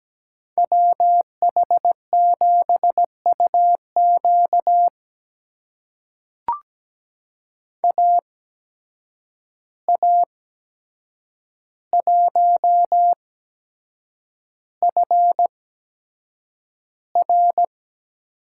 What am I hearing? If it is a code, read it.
Morse code, heard as a keyed tone, about 17 wpm, spaced as for 7 wpm, 700 Hz. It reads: WH7UQ AA1FR
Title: USA Call Signs - 5 Characters - 8x Character Spacing - 17wpm